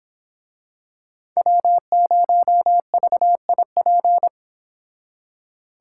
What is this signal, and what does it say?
Morse code, heard as a keyed tone, about 26 wpm, 700 Hz. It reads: W0VIP
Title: Speed Racing - USA 1x3 Call Signs 22wpm